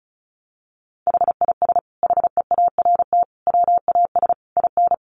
Morse code, heard as a keyed tone, about 34 wpm, 700 Hz. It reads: HIS HEART WAS IN
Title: Sentences from Top 1400 Words 35wpm